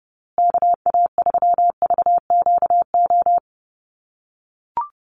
Morse code, heard as a keyed tone, about 29 wpm, 700 Hz. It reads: KA3VQO
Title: USA Call Signs Repeated 2x 30wpm